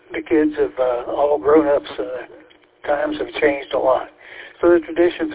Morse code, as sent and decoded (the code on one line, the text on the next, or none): none